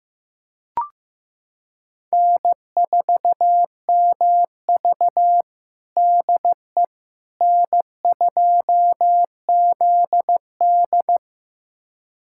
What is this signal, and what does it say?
Morse code, heard as a keyed tone, about 15 wpm, 700 Hz. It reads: N4MV DE N2ZD
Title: USA Callsign DE Callsign - 4 Characters - 15wpm